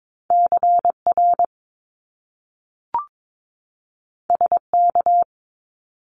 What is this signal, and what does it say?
Morse code, heard as a keyed tone, about 22 wpm, 700 Hz. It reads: CR SK
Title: Top Two Letter Combinations in Words 22wpm